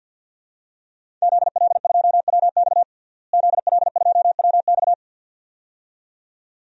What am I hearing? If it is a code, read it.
Morse code, heard as a keyed tone, about 52 wpm, 700 Hz. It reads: ZL2WX ZL2WX
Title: DXCC-100 Call Signs - 5 Characters - Repeated 2x - 50wpm